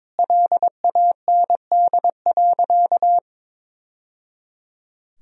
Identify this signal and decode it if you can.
Morse code, heard as a keyed tone, about 22 wpm, 700 Hz. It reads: LAND.
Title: Sentences from Top 1000 Words - Review Part 4 - 22wpm